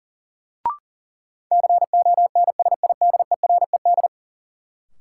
Morse code, heard as a keyed tone, about 40 wpm, 700 Hz. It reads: CONSIDERED
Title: Sets of 5 Words - New Word Drill - 40wpm